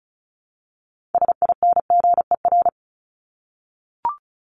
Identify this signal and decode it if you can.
Morse code, heard as a keyed tone, about 34 wpm, 700 Hz. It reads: SINGER
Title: Top 1501-1750 Words 35wpm